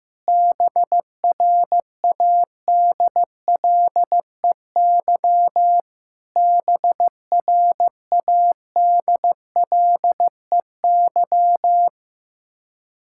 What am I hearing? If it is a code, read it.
Morse code, heard as a keyed tone, about 16 wpm, 700 Hz. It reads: BRADLEY BRADLEY
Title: Practice QSO - Exchange 1 - Beginning - 15wpm